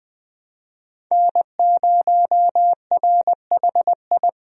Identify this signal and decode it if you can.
Morse code, heard as a keyed tone, about 21 wpm, 700 Hz. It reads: N0RHI
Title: USA Call Signs - 5 Characters - 20wpm